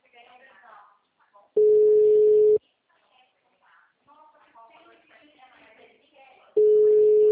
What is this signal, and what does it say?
Ringback tone (Europe, 425 Hz), with speech.
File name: out- -2014- - - .219